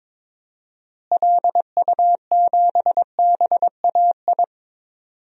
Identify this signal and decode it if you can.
Morse code, heard as a keyed tone, about 22 wpm, 700 Hz. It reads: LU7BAI
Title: DXCC-100 Call Signs - 6 Characters - Repeated 2x - 22wpm